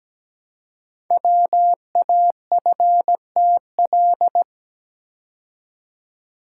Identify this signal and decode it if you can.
Morse code, heard as a keyed tone, about 17 wpm, 700 Hz. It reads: WAFTL
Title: Code Groups - 5 Letters - 17wpm